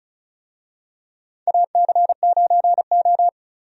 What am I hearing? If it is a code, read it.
Morse code, heard as a keyed tone, about 34 wpm, 700 Hz. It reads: AC9O
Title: USA Call Signs - 4 Characters - 35wpm